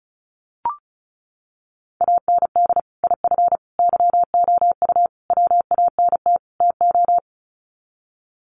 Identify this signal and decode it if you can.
Morse code, heard as a keyed tone, about 34 wpm, 700 Hz. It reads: AND IF YOU WANT TO
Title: Sets of 5 Words Encore 35wpm